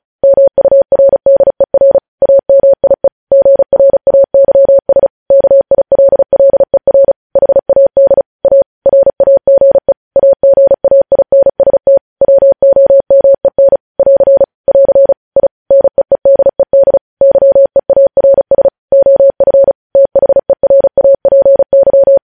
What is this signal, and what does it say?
Morse code, heard as a keyed tone, about 36 wpm, 550 Hz. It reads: MURDER AMIE GRAYS KILLER HAD A RAGE AGAINST WOMEN <AR> <AR> I NEEDED YEARS OF THERAPY
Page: News headlines in Morse code at 35 wpm